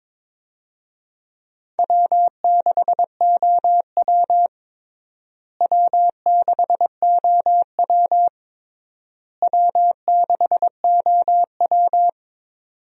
Morse code, heard as a keyed tone, about 22 wpm, 700 Hz. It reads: W6OW W6OW W6OW
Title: USA Call Signs - 4 Characters - Repeated 3x - 22wpm